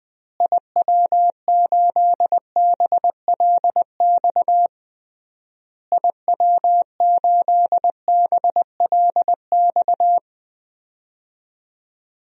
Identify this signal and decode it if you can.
Morse code, heard as a keyed tone, about 20 wpm, 700 Hz. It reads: IW8BLX IW8BLX
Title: DXCC-100 Call Signs - 6 Characters - Repeated 3x - 20wpm